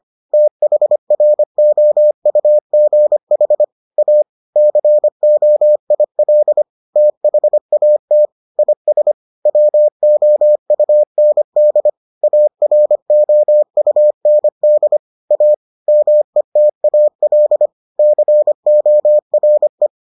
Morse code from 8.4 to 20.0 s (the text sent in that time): IS WOUND AROUND A METAL CORE